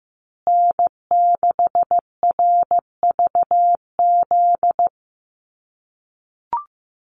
Morse code, heard as a keyed tone, about 15 wpm, 700 Hz. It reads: N6RVZ
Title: USA Call Signs Repeated 3x 15wpm